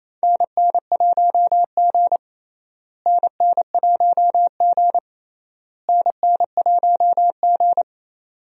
Morse code, heard as a keyed tone, about 28 wpm, 700 Hz. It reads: NN1G NN1G NN1G
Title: USA Call Signs - 4 Characters - Repeated 3x - 28wpm